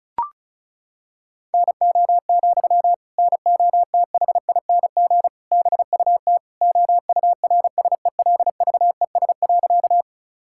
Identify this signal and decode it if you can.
Morse code, heard as a keyed tone, about 34 wpm, 700 Hz. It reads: NO, NOTHING BUT OURSELVES.
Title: Sentences from Top 1600 Words 35wpm